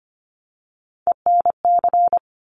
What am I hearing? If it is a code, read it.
Morse code, heard as a keyed tone, about 25 wpm, 700 Hz. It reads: ENC
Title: Top Three Letter Combinations in Words 25wpm